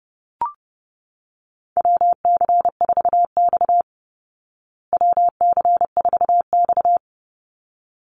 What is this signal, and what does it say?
Morse code, heard as a keyed tone, about 30 wpm, 700 Hz. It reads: WC4X WC4X
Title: Call Signs Repeated 3x - 30wpm